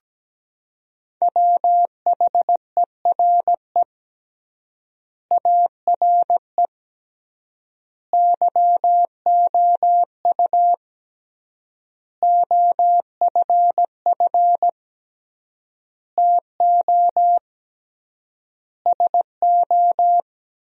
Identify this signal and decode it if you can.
Morse code, heard as a keyed tone, about 17 wpm, 700 Hz. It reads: WHERE ARE YOU OFF TO SO
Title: 3x Word-Spacing - Sentences from Top 400 Words - 17wpm